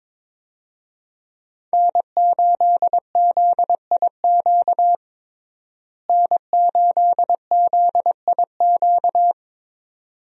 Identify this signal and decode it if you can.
Morse code, heard as a keyed tone, about 22 wpm, 700 Hz. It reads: N8ZIQ N8ZIQ
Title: USA Call Signs - 5 Characters - Repeated 2x - 22wpm